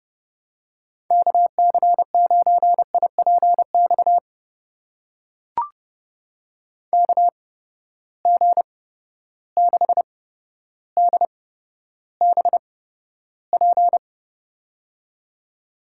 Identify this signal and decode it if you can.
Morse code, heard as a keyed tone, about 30 wpm, spaced as for 13 wpm, 700 Hz. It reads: KC9IPX KG6DBP
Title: USA Call Signs - 6 Characters - 8x Character Spacing - 30wpm